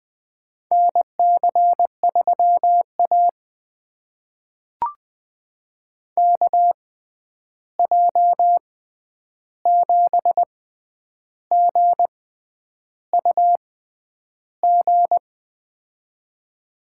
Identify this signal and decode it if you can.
Morse code, heard as a keyed tone, about 20 wpm, spaced as for 9 wpm, 700 Hz. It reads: NC3A KJ7GUG
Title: USA Call Signs - 6x Character Spacing - 20wpm